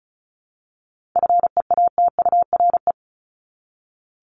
Morse code, heard as a keyed tone, about 34 wpm, 700 Hz. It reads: FEATURE
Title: Top 751-1000 Words 35wpm